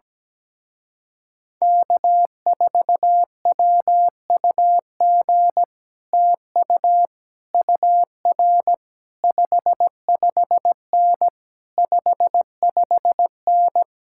Morse code, heard as a keyed tone, about 17 wpm, 700 Hz. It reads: K4WUG TU UR 55N 55N
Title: Speed-Racing Common QSO Phrases - 17wpm